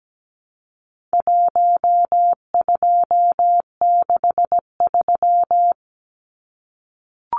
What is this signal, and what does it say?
Morse code, heard as a keyed tone, about 17 wpm, 700 Hz. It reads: 1263
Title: Numbers - 4-Digit 17wpm